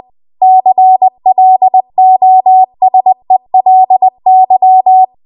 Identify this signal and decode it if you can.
Morse code, heard as a keyed tone, about 20 wpm, 750 Hz. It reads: CLOSELY
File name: Morse / 20WPM